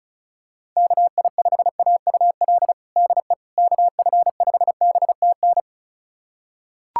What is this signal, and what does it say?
Morse code, heard as a keyed tone, about 36 wpm, 700 Hz. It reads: KI5AUL DE KF5BTN
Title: USA Callsign DE Callsign - 6 Characters - 35wpm